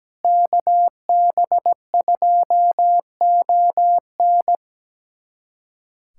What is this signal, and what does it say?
Morse code, heard as a keyed tone, about 17 wpm, 700 Hz. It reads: KB2ON
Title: USA Callsign DE Callsign - 5 Characters - 17wpm